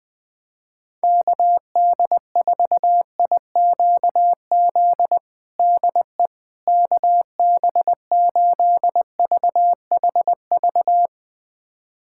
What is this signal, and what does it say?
Morse code, heard as a keyed tone, about 20 wpm, 700 Hz. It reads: KD4IQZ DE KB8VHV
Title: USA Callsign DE Callsign - 6 Characters - 20wpm